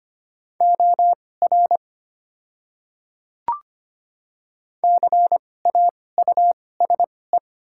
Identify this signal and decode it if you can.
Morse code, heard as a keyed tone, about 25 wpm, spaced as for 18 wpm, 700 Hz. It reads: OR CAUSE
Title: 2x Character Spacing - Top 200 Words - 25wpm